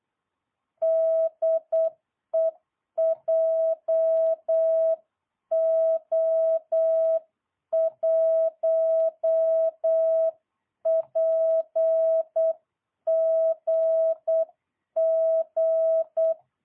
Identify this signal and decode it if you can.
Morse code, heard as a keyed tone, about 8 wpm, 660 Hz. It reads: DE JO1PGG